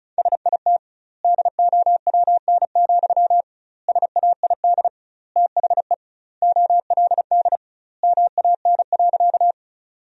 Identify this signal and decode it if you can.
Morse code, heard as a keyed tone, about 36 wpm, 700 Hz. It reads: SIT DOWN, SAID THE OLD MAN.
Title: 2x Word-Spacing - Sentences from Top 500 Words - 35wpm